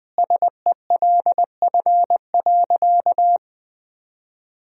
Morse code, heard as a keyed tone, about 20 wpm, 700 Hz. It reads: SELF.
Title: Sentences from Top 1000 Words - Review Part 1 - 20wpm